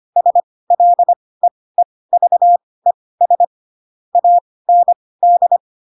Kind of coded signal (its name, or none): Morse code